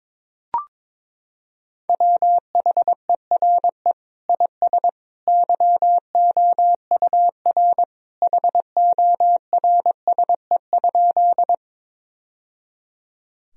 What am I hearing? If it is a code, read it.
Morse code, heard as a keyed tone, about 22 wpm, 700 Hz. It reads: WHERE IS YOUR HORSE?
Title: Sentences from Top 1000 Words - Review Part 1 - 22wpm